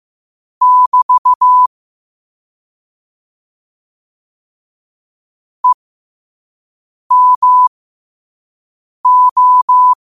Morse code, heard as a keyed tone, about 15 wpm, spaced as for 5 wpm, 1000 Hz. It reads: <BT> EMO
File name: qotdfiles / 5wpm